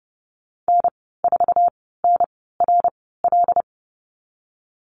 Morse code, heard as a keyed tone, about 30 wpm, spaced as for 17 wpm, 700 Hz. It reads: N4NRL
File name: Lesson 14g - Number 2 - Call Signs 30wpm